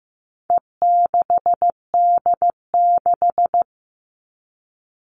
Morse code, heard as a keyed tone, about 15 wpm, 700 Hz. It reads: E6D6
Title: All Cut Numbers - 4-Digit 15wpm